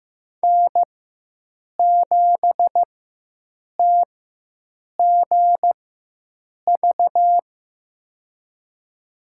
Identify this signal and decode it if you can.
Morse code, heard as a keyed tone, about 15 wpm, spaced as for 7 wpm, 700 Hz. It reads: N7TGV
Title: USA Call Signs - 4x Character Spacing - 15wpm